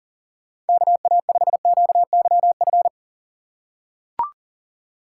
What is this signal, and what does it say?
Morse code, heard as a keyed tone, about 40 wpm, 700 Hz. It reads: KA5QYF
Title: USA Call Signs - 6 Characters - 2x Character Spacing - 40wpm